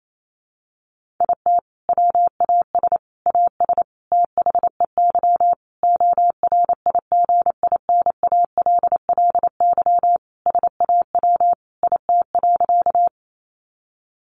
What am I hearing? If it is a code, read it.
Morse code, heard as a keyed tone, about 28 wpm, 700 Hz. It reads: IT WAS AS THEY ORIGINALLY SAW IT.
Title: Sentences from Top 1500 Words 28wpm